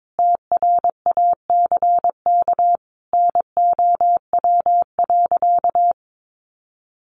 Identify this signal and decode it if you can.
Morse code, heard as a keyed tone, about 22 wpm, 700 Hz. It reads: TRACK NOW.
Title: Sentences from Top 1100 Words 22wpm